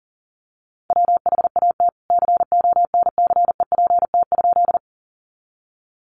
Morse code, heard as a keyed tone, about 40 wpm, 700 Hz. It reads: WHAT CONCEPT?
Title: Sentences from Top 1100 Words 40wpm